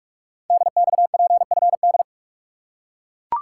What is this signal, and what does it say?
Morse code, heard as a keyed tone, about 44 wpm, 700 Hz. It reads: DXPFD
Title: Code Groups - 5 Letters - 45wpm